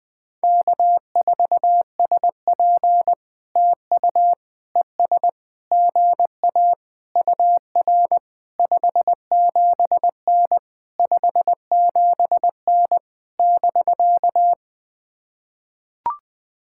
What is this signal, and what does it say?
Morse code, heard as a keyed tone, about 20 wpm, 700 Hz. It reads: K4SP TU ES GA UR 57N 57N <BK>
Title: POTA QSO Practice - 20wpm